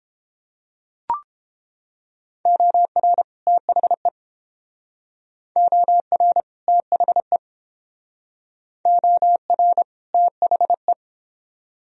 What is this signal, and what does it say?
Morse code, heard as a keyed tone, about 29 wpm, 700 Hz. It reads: OR THE OR THE OR THE